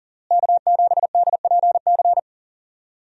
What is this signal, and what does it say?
Morse code, heard as a keyed tone, about 37 wpm, 700 Hz. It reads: K7DPC
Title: USA Call Signs - 5 Characters - Repeated 2x - 40wpm